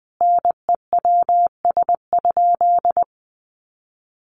Morse code, heard as a keyed tone, about 20 wpm, 700 Hz. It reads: NEWS?
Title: Sentences from Top 800 Words 20wpm